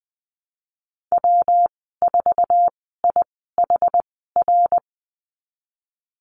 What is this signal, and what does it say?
Morse code, heard as a keyed tone, about 20 wpm, spaced as for 14 wpm, 700 Hz. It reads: W4IHR
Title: 161 Lesson 26g - Letter g - Call Signs 20wpm